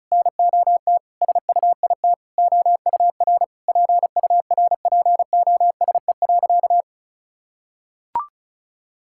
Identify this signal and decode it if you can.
Morse code, heard as a keyed tone, about 34 wpm, 700 Hz. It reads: NOT SUIT OUR PURPOSE.
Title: Sentences from Top 1600 Words 35wpm